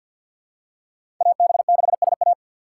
Morse code, heard as a keyed tone, about 53 wpm, 700 Hz. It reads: AB6SA
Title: USA Call Signs - 5 Characters - 3x Character Spacing - 50wpm